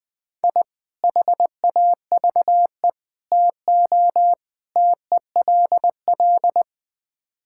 Morse code, heard as a keyed tone, about 19 wpm, 690 Hz. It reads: I HAVE TO TELL
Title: Build A Phrase - 5-Word Phrases - 20wpm